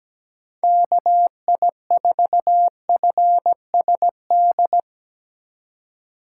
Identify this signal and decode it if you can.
Morse code, heard as a keyed tone, about 17 wpm, 700 Hz. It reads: KI4FSD
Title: USA Call Signs - 6 Characters - 3x Character Spacing - 17wpm